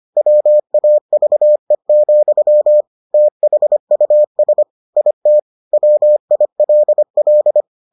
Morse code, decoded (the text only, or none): WAVE, THUS IT WILL